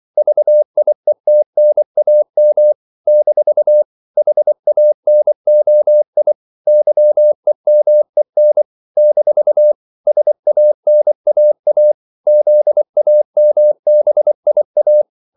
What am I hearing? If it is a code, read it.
Morse code, heard as a keyed tone, about 24 wpm, 600 Hz. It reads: VIETNAM - HANOI YEMEN - SANAA ZAMBIA
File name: CW / Practice / NationalCapitals24